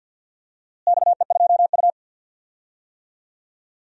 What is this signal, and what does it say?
Morse code, heard as a keyed tone, about 50 wpm, 700 Hz. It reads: XE2U
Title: DXCC-100 Call Signs - 4 Characters - Repeated 3x - 50wpm